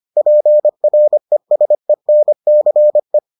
Morse code, heard as a keyed tone, about 25 wpm, 600 Hz. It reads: PRESENCE